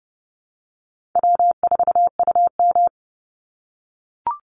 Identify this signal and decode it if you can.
Morse code, heard as a keyed tone, about 30 wpm, 700 Hz. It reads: W4UM
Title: Easy Call Signs - 30wpm